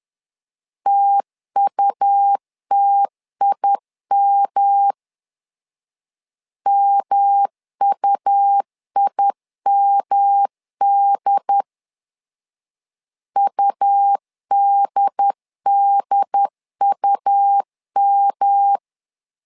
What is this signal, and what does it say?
Morse code, heard as a keyed tone, about 11 wpm, 800 Hz. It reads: TUTIM MUIMD UDDUM
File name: LEZIONI / 3 MWRO / 3 M 10wpm 1828